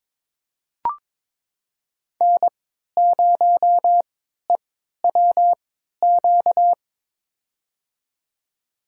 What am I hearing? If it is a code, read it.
Morse code, heard as a keyed tone, about 21 wpm, spaced as for 12 wpm, 700 Hz. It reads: N0EWQ
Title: USA Call Signs - 5 Characters - 3x Character Spacing - 22wpm